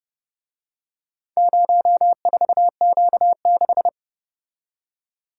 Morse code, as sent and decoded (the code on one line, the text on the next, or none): ----- ....- --.- -....
04Q6